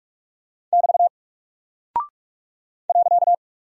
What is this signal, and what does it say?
Morse code, heard as a keyed tone, about 44 wpm, 700 Hz. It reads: <BT> .
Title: Single Character (Letter:Number:Punctuation:Prosign) 45wpm